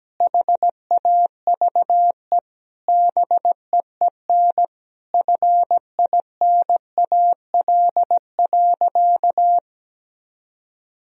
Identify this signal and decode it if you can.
Morse code, heard as a keyed tone, about 17 wpm, 700 Hz. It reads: HAVE BEEN FINAL.
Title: Sentences from Top 1000 Words - Review Part 2 - 17wpm